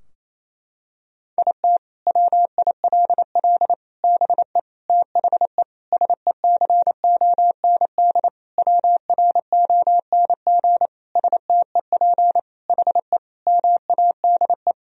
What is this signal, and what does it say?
Morse code, heard as a keyed tone, about 28 wpm, 700 Hz. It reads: IT WILL BE THE SECOND WRONG STEP HE MADE
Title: Sentences from Top 1000 Words - Review Part 2 - 28wpm